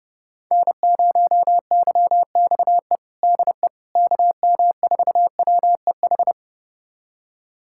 Morse code, heard as a keyed tone, about 31 wpm, 700 Hz. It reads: N0YXE DE KM4WEH
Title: USA Callsign DE Callsign 30wpm